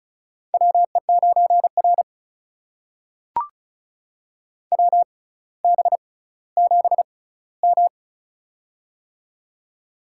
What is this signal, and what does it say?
Morse code, heard as a keyed tone, about 36 wpm, spaced as for 18 wpm, 700 Hz. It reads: WE9R WB7M
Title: USA Call Signs - 4 Characters - 6x Character Spacing - 35wpm